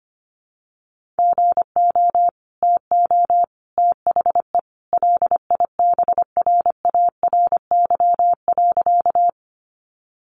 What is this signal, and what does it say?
Morse code, heard as a keyed tone, about 25 wpm, 700 Hz. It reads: GO TO THE LIBRARY.
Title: Sentences from Top 600 Words 25wpm